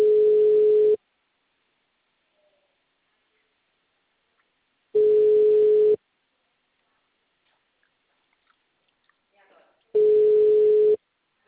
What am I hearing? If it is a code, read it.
Ringback tone (Europe, 425 Hz), with speech.
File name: out- -2020- - - .37626